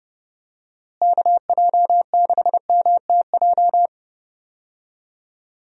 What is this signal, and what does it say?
Morse code, heard as a keyed tone, about 30 wpm, 700 Hz. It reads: KJ6MTJ
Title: USA Call Signs - 6 Characters - Repeated 3x - 30wpm